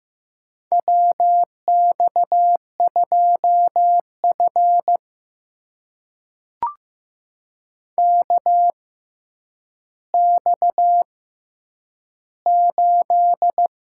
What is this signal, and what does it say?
Morse code, heard as a keyed tone, about 15 wpm, spaced as for 9 wpm, 700 Hz. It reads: WX2F KX8
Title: USA Call Signs - 4 Characters - 6x Character Spacing - 15wpm